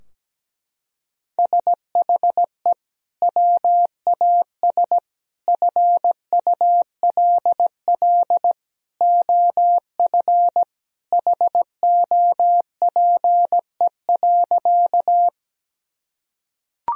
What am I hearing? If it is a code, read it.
Morse code, heard as a keyed tone, about 17 wpm, 700 Hz. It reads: SHE WAS FULL OF HOPE.
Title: Sentences from Top 1000 Words - Review Part 3 - 17wpm